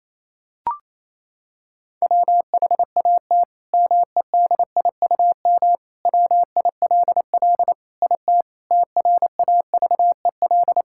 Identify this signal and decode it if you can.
Morse code, heard as a keyed tone, about 28 wpm, 700 Hz. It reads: WHAT MEDIUM WILL IT TRAVEL
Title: Sentences from Top 1000 Words - Review Part 6 - 28wpm